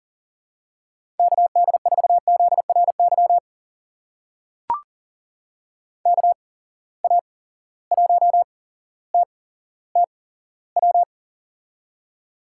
Morse code, heard as a keyed tone, about 42 wpm, spaced as for 17 wpm, 700 Hz. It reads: KD4ZRY KA1TTW
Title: USA Call Signs - 6 Characters - 8x Character Spacing - 40wpm